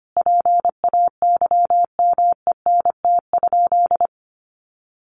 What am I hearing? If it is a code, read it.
Morse code, heard as a keyed tone, about 25 wpm, 700 Hz. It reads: PAYMENT?
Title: Sentences from Top 1200 Words 25wpm